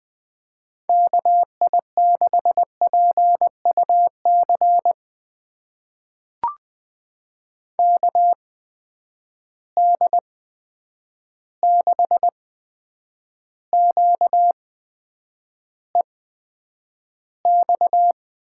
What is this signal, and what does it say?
Morse code, heard as a keyed tone, about 20 wpm, spaced as for 9 wpm, 700 Hz. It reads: KI6PUC KD6QEX
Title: USA Call Signs - 6 Characters - 8x Character Spacing - 20wpm